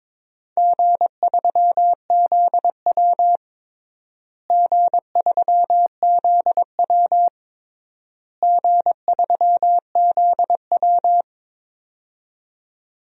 Morse code, heard as a keyed tone, about 22 wpm, 700 Hz. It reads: G3ZW G3ZW G3ZW
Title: DXCC-100 Call Signs - 4 Characters - Repeated 3x - 22wpm